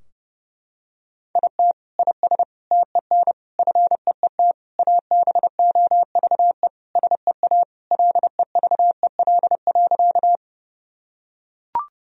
Morse code, heard as a keyed tone, about 30 wpm, 700 Hz. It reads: IT IS TEN FEET ABOVE SEA LEVEL.
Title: Sentences from Top 1000 Words - Review Part 4 - 30wpm